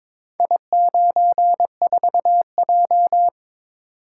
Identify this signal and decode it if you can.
Morse code, heard as a keyed tone, about 22 wpm, 700 Hz. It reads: I94J
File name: practice / 22wpm / Four Letter-Number Combinations - 22wpm